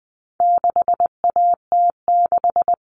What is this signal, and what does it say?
Morse code, heard as a keyed tone, about 20 wpm, 700 Hz. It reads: 6AT6
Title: All Cut Numbers - 4-Digit 20wpm